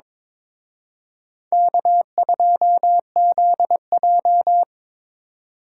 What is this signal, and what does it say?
Morse code, heard as a keyed tone, about 26 wpm, 700 Hz. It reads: K2ZJ
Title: Speed Racing - USA 1x2 Call Signs 22wpm